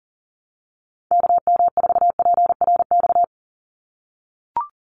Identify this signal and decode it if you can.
Morse code, heard as a keyed tone, about 40 wpm, 690 Hz. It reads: KM4PRX
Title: USA Call Signs 40wpm